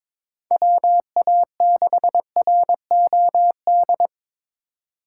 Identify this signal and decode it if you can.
Morse code, heard as a keyed tone, about 22 wpm, 700 Hz. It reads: WA6ROD
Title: USA Call Signs - 2x Character Spacing - 22wpm